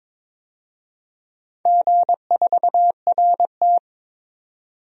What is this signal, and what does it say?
Morse code, heard as a keyed tone, about 22 wpm, 700 Hz. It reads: G4RT
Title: DXCC-100 Call Signs - 4 Characters - 22wpm